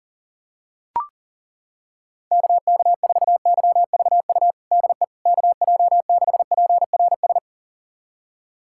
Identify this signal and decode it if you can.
Morse code, heard as a keyed tone, about 37 wpm, 700 Hz. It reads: KK4YVU DE KJ6PRS
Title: USA Callsign DE Callsign 40wpm